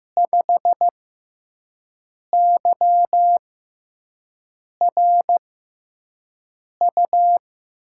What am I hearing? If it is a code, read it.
Morse code, heard as a keyed tone, about 15 wpm, spaced as for 5 wpm, 700 Hz. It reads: 5YRU
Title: USA Call Signs - 6x Character Spacing - 15wpm